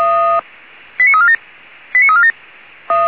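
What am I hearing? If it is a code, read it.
Five-tone selective call (ZVEI), heard as quick runs of five tones, tones of about 70 ms. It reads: 89347 89347
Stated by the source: ZVEI five-tone selective call received via SDR.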